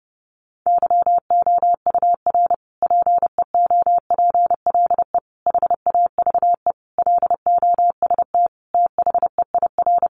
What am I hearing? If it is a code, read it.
Morse code, heard as a keyed tone, about 30 wpm, 700 Hz. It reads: YOUR PEOPLE HAVE LOST THEIR
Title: Sentences from Top 1400 Words 30wpm